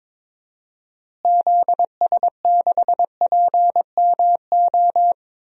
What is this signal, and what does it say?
Morse code, heard as a keyed tone, about 22 wpm, 700 Hz. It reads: ZS6PMO
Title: DXCC-100 Call Signs - 6 Characters - Repeated 2x - 22wpm